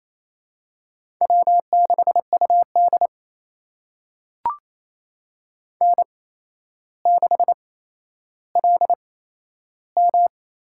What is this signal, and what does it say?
Morse code, heard as a keyed tone, about 28 wpm, spaced as for 12 wpm, 700 Hz. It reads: W6UD N6LM
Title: USA Call Signs - 4 Characters - 8x Character Spacing - 28wpm